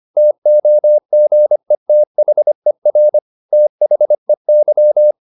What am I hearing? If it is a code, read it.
Morse code, heard as a keyed tone, about 25 wpm, 600 Hz. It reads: TOGETHER THEY
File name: the old man and the sea part 1 25wpm